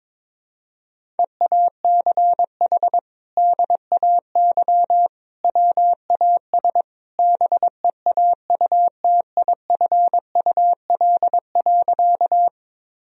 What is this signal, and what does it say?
Morse code, heard as a keyed tone, about 22 wpm, 700 Hz. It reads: EACH DAY WAS BEAUTIFUL.
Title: Sentences from Top 1000 Words - Review Part 1 - 22wpm